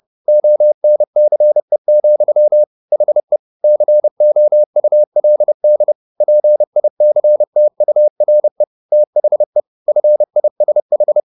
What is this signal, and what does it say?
Morse code, heard as a keyed tone, about 30 wpm, 600 Hz. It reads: ONCE, HE COULD PICTURE THE FISH
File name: the old man and the sea / the old man and the sea 1 30wpm